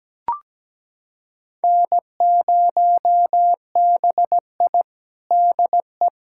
Morse code, heard as a keyed tone, about 17 wpm, 700 Hz. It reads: N0BI DE
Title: USA Callsign DE Callsign - 4 Characters - 17wpm